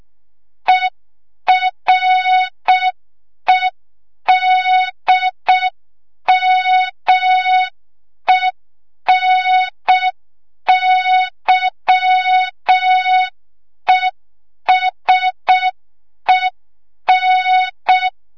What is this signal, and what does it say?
Morse code, heard as a keyed tone, about 6 wpm, 3700 Hz. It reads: EREDMENYESEN
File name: Morze-HA2MN / hang / 25 GYAKORLAT